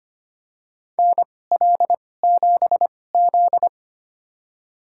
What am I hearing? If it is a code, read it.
Morse code, heard as a keyed tone, about 25 wpm, spaced as for 18 wpm, 700 Hz. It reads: NL7Z
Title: USA Call Signs - 4 Characters - 2x Character Spacing - 25wpm